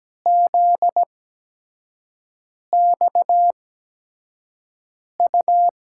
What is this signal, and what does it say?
Morse code, heard as a keyed tone, about 17 wpm, spaced as for 5 wpm, 700 Hz. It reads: ZXU
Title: USA Call Signs - 8x Character Spacing - 17wpm